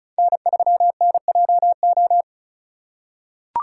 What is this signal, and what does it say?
Morse code, heard as a keyed tone, about 36 wpm, 700 Hz. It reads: N3NJO
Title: USA Call Signs - 5 Characters - 4x Character Spacing - 35wpm